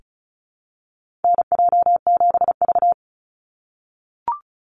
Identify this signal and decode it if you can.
Morse code, heard as a keyed tone, about 34 wpm, 700 Hz. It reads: NJ7V
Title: NA SOTA Chaser Callsigns 35wpm